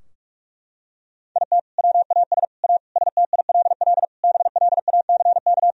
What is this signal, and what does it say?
Morse code, heard as a keyed tone, about 46 wpm, 700 Hz. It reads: IT WAS A STILL BLACK
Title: Sentences from Top 1000 Words - Review Part 2 - 45wpm